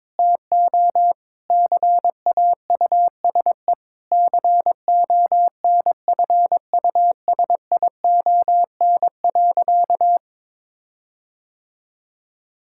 Morse code, heard as a keyed tone, about 22 wpm, 700 Hz. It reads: TO CAUSE CONFUSION.
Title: Sentences from Top 1600 Words 22wpm